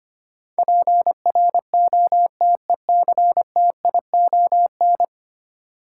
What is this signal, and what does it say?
Morse code, heard as a keyed tone, about 25 wpm, 700 Hz. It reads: PROTECTION
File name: 10 Letter Words - First 3 Letters Given - 25wpm